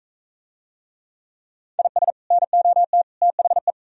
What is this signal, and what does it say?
Morse code, heard as a keyed tone, about 45 wpm, 700 Hz. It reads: IS NOT THE